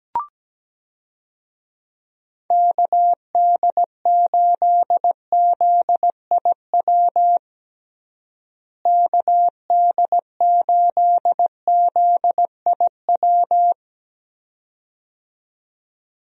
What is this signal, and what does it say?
Morse code, heard as a keyed tone, about 17 wpm, 700 Hz. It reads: KD8ZIW KD8ZIW
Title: USA Call Signs - 6 Characters - Repeated 2x - 17wpm